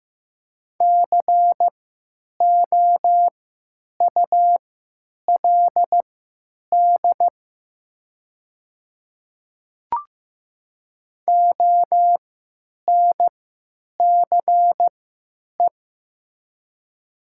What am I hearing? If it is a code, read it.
Morse code, heard as a keyed tone, about 15 wpm, spaced as for 9 wpm, 700 Hz. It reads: COULD ONCE
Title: 3x Character Spacing - Top 300 Words - 15wpm